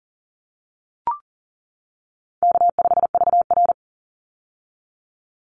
Morse code, heard as a keyed tone, about 40 wpm, 700 Hz. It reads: K5VR
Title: Easy Call Signs - 40wpm